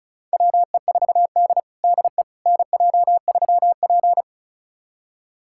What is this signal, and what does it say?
Morse code, heard as a keyed tone, about 36 wpm, 700 Hz. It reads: WE4D DE NJ3P
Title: USA Callsign DE Callsign - 4 Characters - 35wpm